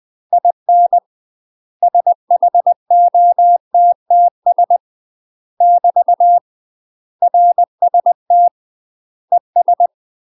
Morse code, heard as a keyed tone, about 20 wpm, 700 Hz. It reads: IN SHOTTS <BT> RST ES
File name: UK MASTER QSOs TRAINING / 20WPM / UK MASTER QSOs 0060